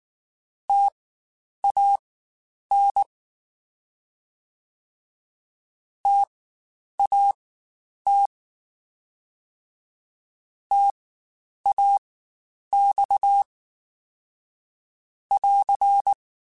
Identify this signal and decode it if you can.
Morse code, heard as a keyed tone, about 19 wpm, spaced as for 9 wpm, 800 Hz. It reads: TAN TAT TAX <AR>
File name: Spaced Alphas / Spaced Alphas mp3s / Ts sp @18